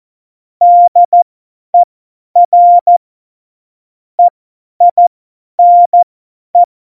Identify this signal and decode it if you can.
Morse code, heard as a keyed tone, about 14 wpm, spaced as for 10 wpm, 700 Hz. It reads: DER EINE
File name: Morsekurs / Witze / Witz 11 August 2020, 50 F70